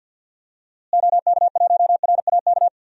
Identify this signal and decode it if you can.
Morse code, heard as a keyed tone, about 52 wpm, 700 Hz. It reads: OK1RAK
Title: DXCC-100 Call Signs - 6 Characters - Repeated 3x - 50wpm